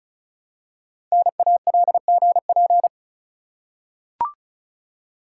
Morse code, heard as a keyed tone, about 36 wpm, 690 Hz. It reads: NALGP
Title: Code Groups - 5 Letters - 35wpm